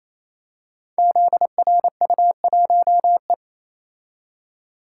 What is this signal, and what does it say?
Morse code, heard as a keyed tone, about 27 wpm, 700 Hz. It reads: ZRU1E
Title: Code Groups - 5 Letter:Numbers - 28wpm